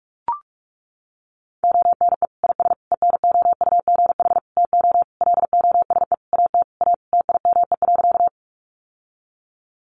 Morse code, heard as a keyed tone, about 44 wpm, 700 Hz. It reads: ONE IS ENOUGH TO LOSE AT A TIME.
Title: Sentences from Top 1100 Words 45wpm